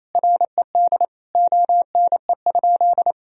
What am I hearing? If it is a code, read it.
Morse code, heard as a keyed tone, about 28 wpm, 700 Hz. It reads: RED ONE?
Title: Sentences from Top 1000 Words - Review Part 2 - 28wpm